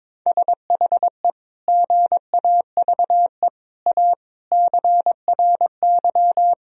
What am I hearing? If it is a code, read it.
Morse code, heard as a keyed tone, about 22 wpm, 700 Hz. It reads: SHE GAVE A CRY
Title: Sentences from Top 1000 Words - Review Part 3 - 22wpm